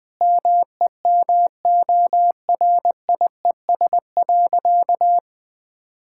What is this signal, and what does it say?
Morse code, heard as a keyed tone, about 20 wpm, 700 Hz. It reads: MEMORIES.
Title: Difficult Sentences with Hints - 20wpm